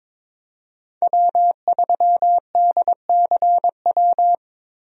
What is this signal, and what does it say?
Morse code, heard as a keyed tone, about 22 wpm, 700 Hz. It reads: W3DCW
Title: USA Call Signs - 5 Characters - Repeated 3x - 22wpm